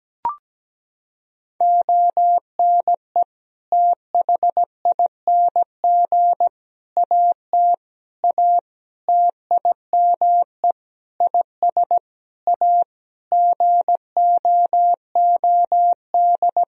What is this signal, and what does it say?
Morse code, heard as a keyed tone, about 17 wpm, 700 Hz. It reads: ONE THING AT A TIME IS A GOOD
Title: Sentences from Top 1000 Words - Review Part 2 - 17wpm